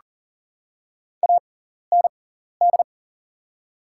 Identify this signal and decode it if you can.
Morse code, heard as a keyed tone, about 40 wpm, spaced as for 14 wpm, 700 Hz. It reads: AND
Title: 6x Character Spacing - Top 100 Words - 40wpm